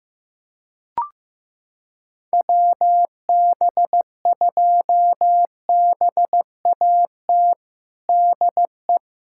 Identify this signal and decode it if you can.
Morse code, heard as a keyed tone, about 15 wpm, 700 Hz. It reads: WB2BAT DE
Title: USA Callsign DE Callsign - 6 Characters - 15wpm